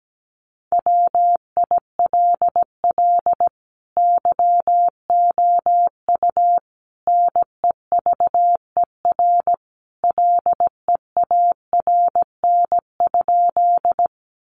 Morse code, heard as a keyed tone, about 17 wpm, 700 Hz. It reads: WILL YOU NEVER LEARN?
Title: Sentences from Top 300 Words 17wpm